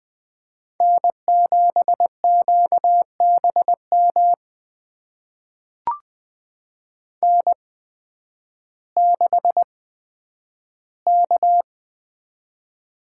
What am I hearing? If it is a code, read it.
Morse code, heard as a keyed tone, about 20 wpm, spaced as for 11 wpm, 700 Hz. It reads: N7QBM N6K
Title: USA Call Signs - 5 Characters - 8x Character Spacing - 20wpm